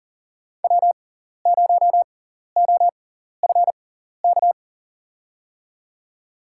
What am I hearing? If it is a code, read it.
Morse code, heard as a keyed tone, about 40 wpm, spaced as for 14 wpm, 700 Hz. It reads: W0OFK
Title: USA Call Signs - 6x Character Spacing - 40wpm